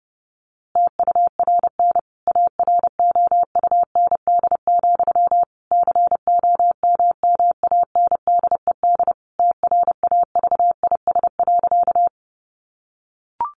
Morse code, heard as a keyed tone, about 30 wpm, 700 Hz. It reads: TURN AROUND, COMMANDED TRAVIS.
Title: A Sound Of Thunder 30wpm